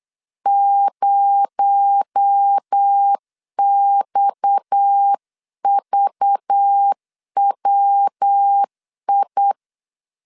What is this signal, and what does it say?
Morse code, heard as a keyed tone, about 8 wpm, 800 Hz. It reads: 0XVWI
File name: 8wpm